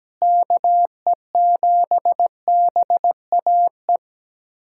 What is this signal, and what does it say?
Morse code, heard as a keyed tone, about 17 wpm, 700 Hz. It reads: KE7BAE
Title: USA Callsign DE Callsign 17wpm